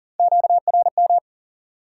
Morse code, heard as a keyed tone, about 40 wpm, 700 Hz. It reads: QRM
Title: Top 90 QSO Elements 40wpm